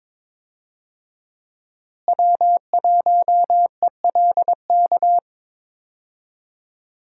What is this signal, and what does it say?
Morse code, heard as a keyed tone, about 22 wpm, 700 Hz. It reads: W1ELK
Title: USA Call Signs - 5 Characters - Repeated 3x - 22wpm